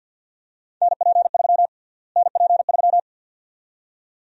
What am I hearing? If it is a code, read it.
Morse code, heard as a keyed tone, about 52 wpm, 700 Hz. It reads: NP3 NP3
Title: USA Call Sign Prefixes - Repeated 2x - 50wpm